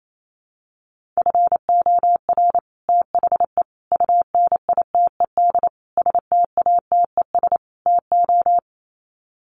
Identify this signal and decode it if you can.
Morse code, heard as a keyed tone, about 28 wpm, 700 Hz. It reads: FOR THE UNITED STATES TO
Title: Sets of 5 Words Encore 28wpm